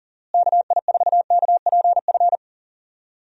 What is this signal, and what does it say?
Morse code, heard as a keyed tone, about 40 wpm, 700 Hz. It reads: KI4KPF
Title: USA Call Signs - 6 Characters - Repeated 3x - 40wpm